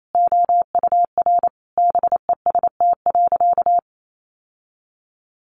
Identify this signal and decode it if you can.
Morse code, heard as a keyed tone, about 28 wpm, 700 Hz. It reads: OUR BEST.
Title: Sentences from Top 800 Words 28wpm